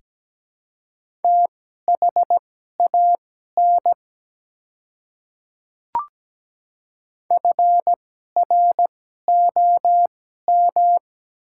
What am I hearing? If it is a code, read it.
Morse code, heard as a keyed tone, about 17 wpm, spaced as for 12 wpm, 700 Hz. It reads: THAN FROM
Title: 2x Character Spacing - Top 100 Words - 17wpm